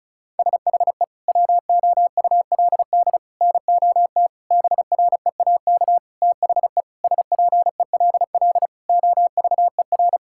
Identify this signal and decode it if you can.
Morse code, heard as a keyed tone, about 34 wpm, 700 Hz. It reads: SHE WOULD NOT BREAK THE SPELL OVER
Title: Sentences from Top 1000 Words - Review Part 6 - 35wpm